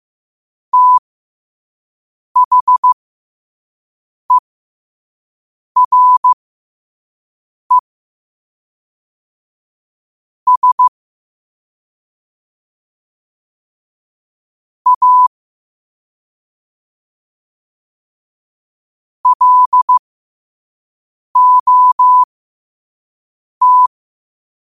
Morse code, heard as a keyed tone, about 15 wpm, spaced as for 5 wpm, 1000 Hz. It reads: THERE S A LOT
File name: qotdfiles / 5wpm